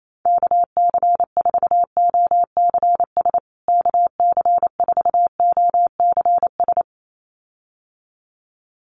Morse code, heard as a keyed tone, about 28 wpm, 700 Hz. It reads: KC4OCS KC4OCS
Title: USA Call Signs Repeated 2x 28wpm